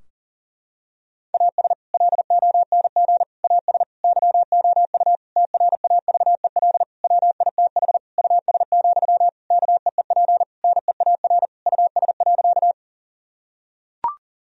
Morse code, heard as a keyed tone, about 40 wpm, 700 Hz. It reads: AS LONG AS YOU TRAVEL WITH US, KEEP NEAR US.
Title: Sentences from Top 1000 Words - Review Part 2 - 40wpm